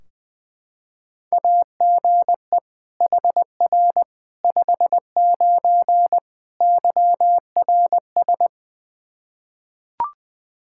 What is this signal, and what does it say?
Morse code, heard as a keyed tone, about 21 wpm, 700 Hz. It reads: AGE HR 59 YRS
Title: Practice QSO - Exchange 3 - More Info - 20wpm